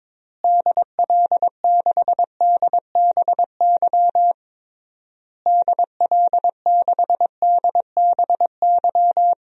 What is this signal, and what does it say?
Morse code, heard as a keyed tone, about 22 wpm, 700 Hz. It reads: DL6DBY DL6DBY
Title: DXCC-100 Call Signs - 6 Characters - Repeated 3x - 22wpm